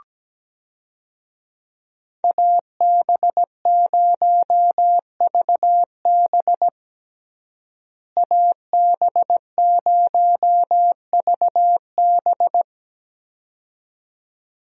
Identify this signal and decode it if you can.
Morse code, heard as a keyed tone, about 17 wpm, 700 Hz. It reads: AB0VB AB0VB
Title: USA Call Signs - 5 Characters - Repeated 2x - 17wpm